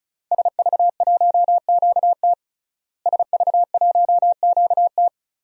Morse code, heard as a keyed tone, about 36 wpm, 700 Hz. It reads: SV1QT SV1QT
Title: DXCC-100 Call Signs - 5 Characters - Repeated 3x - 35wpm